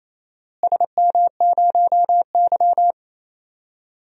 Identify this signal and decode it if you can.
Morse code, heard as a keyed tone, about 28 wpm, 700 Hz. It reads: SM0Y
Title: DXCC-100 Call Signs - 4 Characters - Repeated 2x - 28wpm